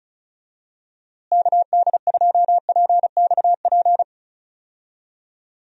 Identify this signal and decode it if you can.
Morse code, heard as a keyed tone, about 36 wpm, 700 Hz. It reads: KD2PXP
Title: USA Call Signs - 6 Characters - Repeated 3x - 35wpm